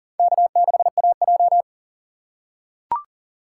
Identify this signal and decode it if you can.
Morse code, heard as a keyed tone, about 40 wpm, 700 Hz. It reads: K6AJ
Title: USA Call Signs - 4 Characters - 4x Character Spacing - 40wpm